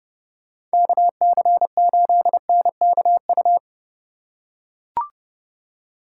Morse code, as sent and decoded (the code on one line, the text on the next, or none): -.- -.-. ---.. -. -.- ..-
KC8NKU